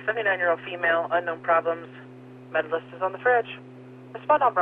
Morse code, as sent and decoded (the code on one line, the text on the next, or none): none